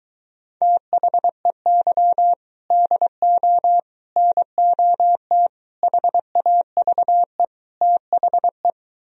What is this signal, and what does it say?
Morse code, heard as a keyed tone, about 23 wpm, 700 Hz. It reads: THEY DO NOT HAVE THE